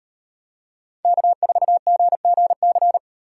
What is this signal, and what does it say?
Morse code, heard as a keyed tone, about 36 wpm, 700 Hz. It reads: K4GGC